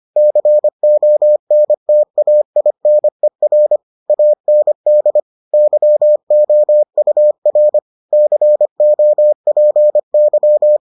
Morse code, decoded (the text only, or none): CONTAINER AND YOUR COPY